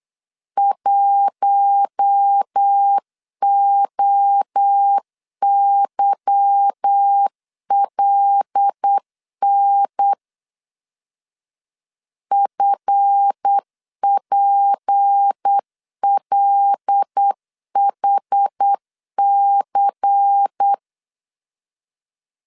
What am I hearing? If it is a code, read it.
Morse code, heard as a keyed tone, about 8 wpm, 800 Hz. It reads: 1OYLN FPLHC
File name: LEZIONI / 8 YJ45 / 8 Y 8wpm 3245